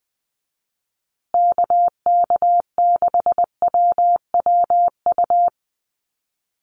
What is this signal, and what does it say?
Morse code, heard as a keyed tone, about 20 wpm, 700 Hz. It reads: KK6WWU
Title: USA Call Signs Repeated 3x 20wpm